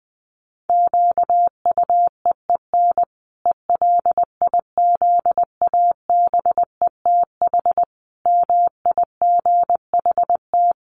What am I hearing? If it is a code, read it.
Morse code, heard as a keyed tone, about 20 wpm, 700 Hz. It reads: QUEEN ELIZABETH MIGHT
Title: A Sound Of Thunder 20wpm